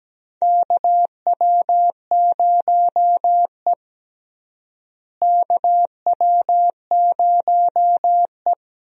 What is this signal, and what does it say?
Morse code, heard as a keyed tone, about 17 wpm, 700 Hz. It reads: KW0E KW0E
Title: USA Call Signs - 4 Characters - Repeated 3x - 17wpm